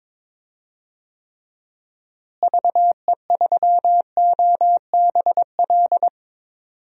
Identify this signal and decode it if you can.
Morse code, heard as a keyed tone, about 22 wpm, 700 Hz. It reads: VE3OBL
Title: DXCC-100 Call Signs - 6 Characters - Repeated 2x - 22wpm